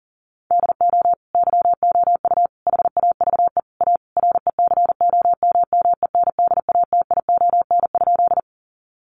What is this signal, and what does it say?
Morse code, heard as a keyed tone, about 40 wpm, 700 Hz. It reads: DO YOU HAVE A RECOMMENDATION?
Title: Sentences from Top 1400 Words 40wpm